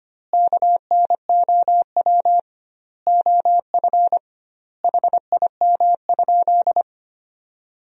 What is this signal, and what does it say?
Morse code, heard as a keyed tone, about 25 wpm, 700 Hz. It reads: KNOW OF HIM?
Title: 2x Word-Spacing - Sentences from Top 100 Words - 25wpm